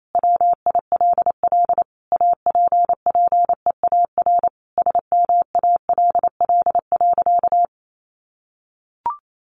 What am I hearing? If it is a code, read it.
Morse code, heard as a keyed tone, about 28 wpm, 700 Hz. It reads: WILL APPEAR SMALL.
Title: Sentences from Top 1500 Words 28wpm